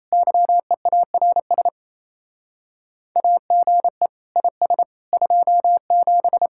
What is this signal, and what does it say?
Morse code, heard as a keyed tone, about 33 wpm, 700 Hz. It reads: YEARS AGE IS 27
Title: Speed-Racing Common QSO Phrases - 28wpm